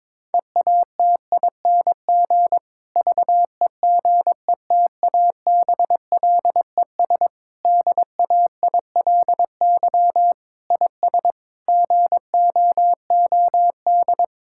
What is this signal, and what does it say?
Morse code, heard as a keyed tone, about 22 wpm, 700 Hz. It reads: EATING VEGETABLES DAILY IS GOOD
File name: Difficult Sentences with Hints - 22wpm